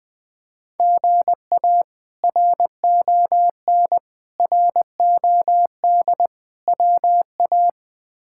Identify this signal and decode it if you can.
Morse code, heard as a keyed tone, about 20 wpm, 700 Hz. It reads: GA RON ROD WA
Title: K1USN SST QSO Practice - 20wpm